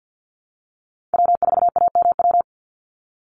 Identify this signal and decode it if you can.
Morse code, heard as a keyed tone, about 48 wpm, 700 Hz. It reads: W4AMW
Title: NA SOTA Chaser Callsigns 50wpm